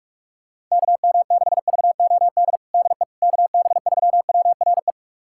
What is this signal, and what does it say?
Morse code, heard as a keyed tone, about 44 wpm, 700 Hz. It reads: KM6VOD DE KB3WRE
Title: USA Callsign DE Callsign 45wpm